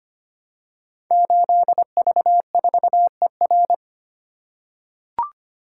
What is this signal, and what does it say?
Morse code, heard as a keyed tone, about 25 wpm, 700 Hz. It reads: 8V4ER
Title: Code Groups - 5 Letter:Numbers - 25wpm